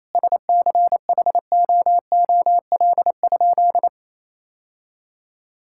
Morse code, heard as a keyed tone, about 28 wpm, 700 Hz. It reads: SCHOOL?
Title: Sentences from Top 1000 Words - Review Part 5 - 28wpm